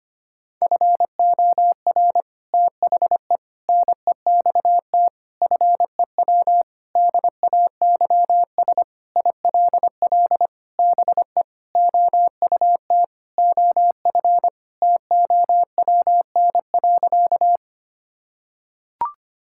Morse code, heard as a keyed tone, about 25 wpm, 700 Hz. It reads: FOR THE NEXT FEW DAYS ILL BE OUT OF TOWN.
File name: practice / 25wpm / Sentences with 3-Word Phrases - 25wpm